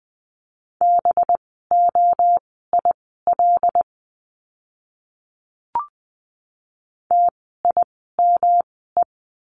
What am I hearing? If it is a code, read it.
Morse code, heard as a keyed tone, about 20 wpm, spaced as for 14 wpm, 700 Hz. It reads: BOIL TIME
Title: 185 Lesson 30f - Letter b - Words 20wpm